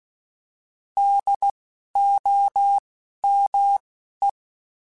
Morse code, heard as a keyed tone, about 16 wpm, 800 Hz. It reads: D O M E
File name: Alphabetical / Alphabetical mp3s / Ds @10